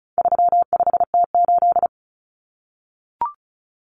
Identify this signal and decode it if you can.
Morse code, heard as a keyed tone, about 34 wpm, 700 Hz. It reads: 35T8
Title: Common Cut Numbers - 4-Digit 35wpm